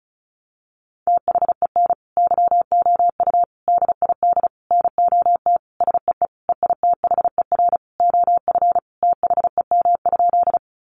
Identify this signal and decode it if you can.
Morse code, heard as a keyed tone, about 34 wpm, 700 Hz. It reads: THEN YOU DID NOT SEE EITHER OF THEM?
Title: Sentences from Top 600 Words 35wpm